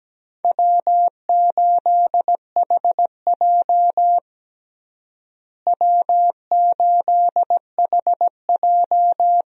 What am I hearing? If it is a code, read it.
Morse code, heard as a keyed tone, about 17 wpm, 700 Hz. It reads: W8HJ W8HJ
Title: USA Call Signs - 4 Characters - Repeated 2x - 17wpm